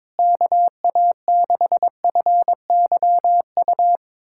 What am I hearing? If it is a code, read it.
Morse code, heard as a keyed tone, about 21 wpm, 700 Hz. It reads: KA6FYU
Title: USA Callsign DE Callsign - 6 Characters - 22wpm